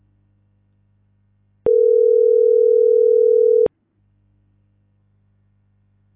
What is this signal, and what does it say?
Ringback tone (North America), 440 + 480 Hz.